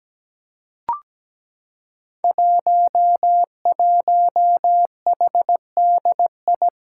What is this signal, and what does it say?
Morse code, heard as a keyed tone, about 17 wpm, 700 Hz. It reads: 11HDI
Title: Five Challenging Letter-Number Combinations 17wpm